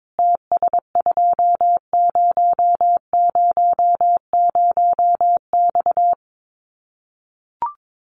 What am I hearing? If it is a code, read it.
Morse code, heard as a keyed tone, about 22 wpm, 700 Hz. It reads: TS2000X
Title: Common QSO Phrases 22wpm